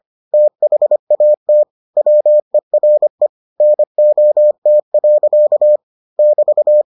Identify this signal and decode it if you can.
Morse code, heard as a keyed tone, about 25 wpm, 600 Hz. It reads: THAT WERE NOT. <BT>